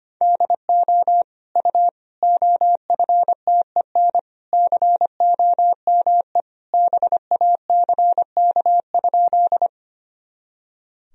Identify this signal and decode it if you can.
Morse code, heard as a keyed tone, about 25 wpm, 700 Hz. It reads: DO U OFTEN COME BACK?
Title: Sentences from Top 300 Words and Abbreviations 25wpm